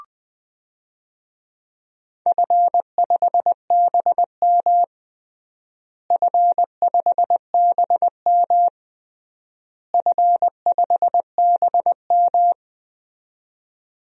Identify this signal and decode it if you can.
Morse code, heard as a keyed tone, about 20 wpm, 700 Hz. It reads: F5BM F5BM F5BM
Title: DXCC-100 Call Signs - 4 Characters - Repeated 3x - 20wpm